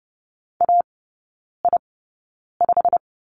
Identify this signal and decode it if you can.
Morse code, heard as a keyed tone, about 30 wpm, spaced as for 9 wpm, 700 Hz. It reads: AI5
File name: course / 30:10wpm / 087 Lesson 15e - Number 5 - Three Characters 30wpm